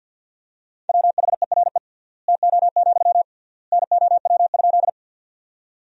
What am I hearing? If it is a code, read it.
Morse code, heard as a keyed tone, about 52 wpm, 700 Hz. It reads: WHERE TO, NOW?
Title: 3x Word-Spacing - Sentences from Top 200 Words - 50wpm